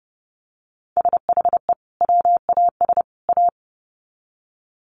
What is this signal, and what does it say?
Morse code, heard as a keyed tone, about 30 wpm, 700 Hz. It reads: SHE WAS A
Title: Sets of 3 Words 30wpm